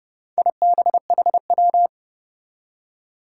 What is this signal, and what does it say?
Morse code, heard as a keyed tone, about 30 wpm, 700 Hz. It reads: IBHW
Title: Four Challenging Letter-Number Combinations 30wpm